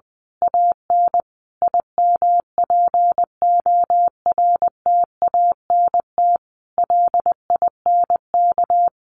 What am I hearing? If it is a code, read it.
Morse code, heard as a keyed tone, about 20 wpm, 700 Hz. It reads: AN IMPORTANT LINK
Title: A Sound Of Thunder 20wpm